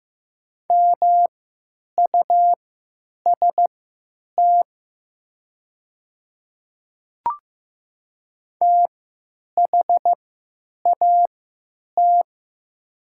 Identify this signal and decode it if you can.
Morse code, heard as a keyed tone, about 15 wpm, spaced as for 9 wpm, 700 Hz. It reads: MUST THAT
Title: 3x Character Spacing - Top 300 Words - 15wpm